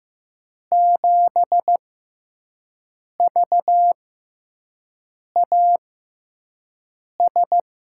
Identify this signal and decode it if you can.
Morse code, heard as a keyed tone, about 15 wpm, spaced as for 5 wpm, 700 Hz. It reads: 7VAS
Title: USA Call Signs - 5 Characters - 6x Character Spacing - 15wpm